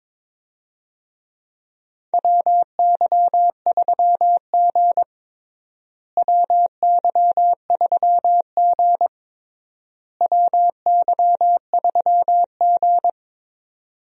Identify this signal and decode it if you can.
Morse code, heard as a keyed tone, about 23 wpm, 700 Hz. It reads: WY3G WY3G WY3G
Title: USA Call Signs - 4 Characters - Repeated 3x - 22wpm